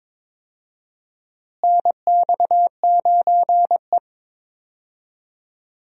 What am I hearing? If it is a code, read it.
Morse code, heard as a keyed tone, about 22 wpm, 700 Hz. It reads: NX9E
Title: USA Call Signs - 4 Characters - Repeated 3x - 22wpm